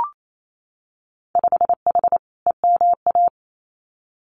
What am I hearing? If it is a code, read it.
Morse code, heard as a keyed tone, about 28 wpm, 700 Hz. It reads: 5H EMA
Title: Winter Field Day Exchanges Encore 28wpm